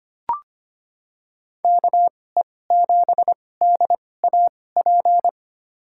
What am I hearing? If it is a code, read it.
Morse code, heard as a keyed tone, about 25 wpm, spaced as for 18 wpm, 700 Hz. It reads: KE7DAP
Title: USA Call Signs - 2x Character Spacing - 25wpm